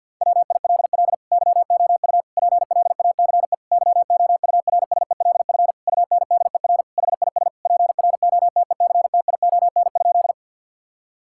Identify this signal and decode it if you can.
Morse code, heard as a keyed tone, about 50 wpm, 700 Hz. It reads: WILL YOU PLACE YOURSELF UNDER HIS PROTECTION?
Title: Sentences from Top 1600 Words 50wpm